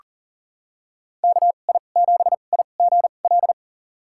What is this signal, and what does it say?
Morse code, heard as a keyed tone, about 37 wpm, spaced as for 28 wpm, 700 Hz. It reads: KI7IGL
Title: USA Call Signs - 6 Characters - 2x Character Spacing - 40wpm